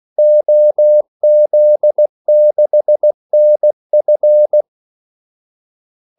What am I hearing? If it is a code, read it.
Morse code, heard as a keyed tone, about 16 wpm, 600 Hz. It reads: OZ6NF
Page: Kaldesignaler 16 wpm